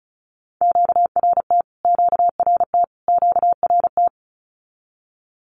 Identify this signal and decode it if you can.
Morse code, heard as a keyed tone, about 34 wpm, 700 Hz. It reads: QRT QRT QRT
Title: Q-Codes Repeated 3x 35wpm